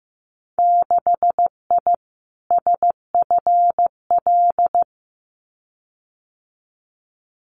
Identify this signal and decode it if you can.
Morse code, heard as a keyed tone, about 15 wpm, 700 Hz. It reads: 6I SFL
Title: Winter Field Day Exchanges 15wpm